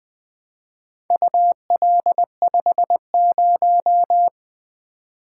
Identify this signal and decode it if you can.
Morse code, heard as a keyed tone, about 20 wpm, 700 Hz. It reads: UL50
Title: Four Letter-Number Combinations - 20wpm